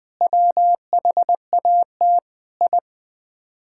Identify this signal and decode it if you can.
Morse code, heard as a keyed tone, about 20 wpm, 700 Hz. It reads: WHAT I
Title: Build A Phrase - 4-Word Phrases - 20wpm